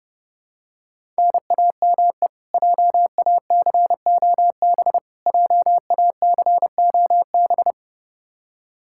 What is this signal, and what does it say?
Morse code, heard as a keyed tone, about 30 wpm, 700 Hz. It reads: NAME JACOB JACOB
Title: K1USN SST QSO Practice 30wpm